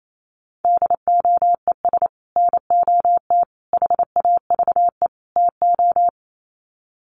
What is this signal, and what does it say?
Morse code, heard as a keyed tone, about 28 wpm, 700 Hz. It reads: DOES NOT HAVE TO
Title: Sets of 4 Words 28wpm